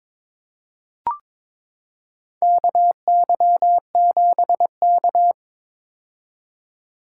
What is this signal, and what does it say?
Morse code, heard as a keyed tone, about 22 wpm, 700 Hz. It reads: KY7K
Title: K1USN SST QSO Practice - 22wpm